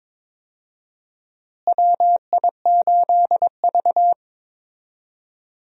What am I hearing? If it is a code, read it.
Morse code, heard as a keyed tone, about 22 wpm, 700 Hz. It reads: WI8V
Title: USA Call Signs - 4 Characters - Repeated 3x - 22wpm